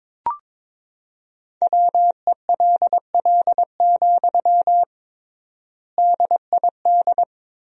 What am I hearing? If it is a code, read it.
Morse code, heard as a keyed tone, about 21 wpm, 700 Hz. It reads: WELL, DID
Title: 3x Word-Spacing - Sentences from Top 200 Words - 22wpm